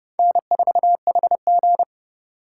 Morse code, heard as a keyed tone, about 32 wpm, 700 Hz. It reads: N4HG
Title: USA Call Signs - 4 Characters - 2x Character Spacing - 30wpm